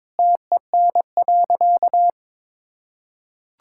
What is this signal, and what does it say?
Morse code, heard as a keyed tone, about 22 wpm, 700 Hz. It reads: TEN.
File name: practice / 22wpm / Easy Sentences with Hints - 22wpm